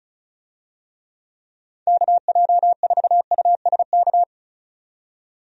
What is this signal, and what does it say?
Morse code, heard as a keyed tone, about 36 wpm, 700 Hz. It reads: KJ4USK
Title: USA Call Signs - 6 Characters - 35wpm